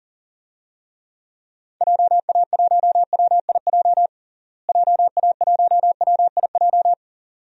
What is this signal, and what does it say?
Morse code, heard as a keyed tone, about 40 wpm, 700 Hz. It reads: JA1WIJ JA1WIJ
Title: DXCC-100 Call Signs - 6 Characters - Repeated 2x - 40wpm